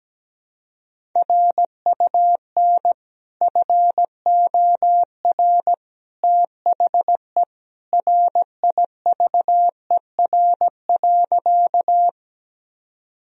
Morse code, heard as a keyed tone, about 17 wpm, 700 Hz. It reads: RUN FOR THE RIVER.
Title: Sentences from Top 1000 Words - Review Part 1 - 17wpm